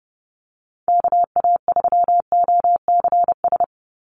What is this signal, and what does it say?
Morse code, heard as a keyed tone, about 30 wpm, 700 Hz. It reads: KA3OCS
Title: Call Signs - 30wpm